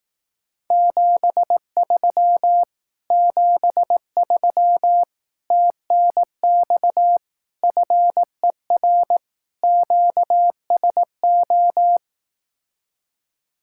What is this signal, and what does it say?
Morse code, heard as a keyed tone, about 18 wpm, 700 Hz. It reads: 73 73 TNX FER QSO
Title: Speed-Racing Common QSO Phrases Encore - 15wpm